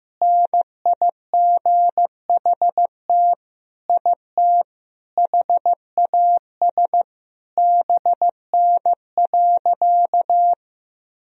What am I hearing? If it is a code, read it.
Morse code, heard as a keyed tone, about 15 wpm, 700 Hz. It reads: NIGHT IT HAS BN.
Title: Sentences from Top 300 Words and Abbreviations 15wpm